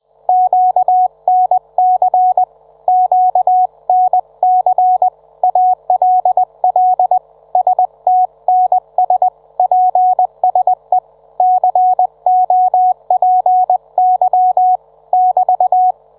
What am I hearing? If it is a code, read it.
Morse code, heard as a keyed tone, about 20 wpm, 740 Hz. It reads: QNC QNC ALL STNS PSE COPY <BT>